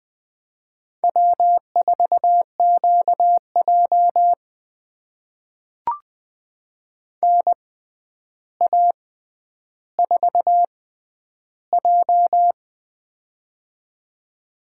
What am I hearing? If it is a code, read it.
Morse code, heard as a keyed tone, about 21 wpm, spaced as for 10 wpm, 700 Hz. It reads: W4QJ NA4J
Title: USA Call Signs - 4 Characters - 6x Character Spacing - 20wpm